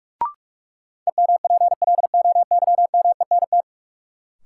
Morse code, heard as a keyed tone, about 42 wpm, 700 Hz. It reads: EMPLOYMENT
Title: Sentences from Top 1500 Words - New Word Drill Alt - 45wpm